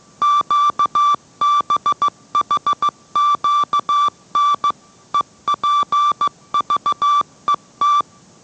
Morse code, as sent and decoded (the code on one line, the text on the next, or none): --.- -... .... --.- -. / . .--. ...- . -
QBHQN EPVET